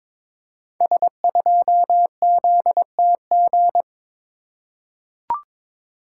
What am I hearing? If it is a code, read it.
Morse code, heard as a keyed tone, about 22 wpm, 700 Hz. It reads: S2ZTG
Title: Code Groups - 5 Letter:Numbers - 22wpm